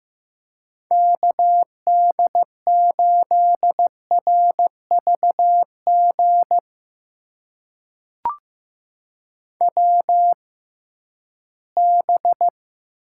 Morse code, heard as a keyed tone, about 15 wpm, spaced as for 11 wpm, 700 Hz. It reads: KD8RVG WB
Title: USA Call Signs - 6 Characters - 6x Character Spacing - 15wpm